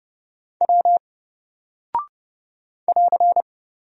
Morse code, heard as a keyed tone, about 30 wpm, 700 Hz. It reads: W <AR>
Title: Single Character (Letter:Number:Punctuation:Prosign) Alt 30wpm